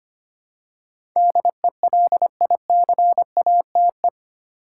Morse code, heard as a keyed tone, about 25 wpm, 700 Hz. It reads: DELICATE
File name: 8 Letter Words - First 3 Letters Given - 25wpm